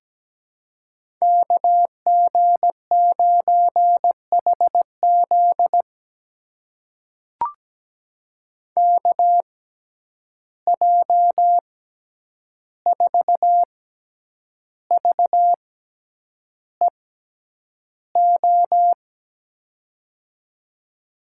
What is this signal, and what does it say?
Morse code, heard as a keyed tone, about 17 wpm, spaced as for 8 wpm, 700 Hz. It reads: KG9HZ KJ4VEO
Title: USA Call Signs - 6x Character Spacing - 17wpm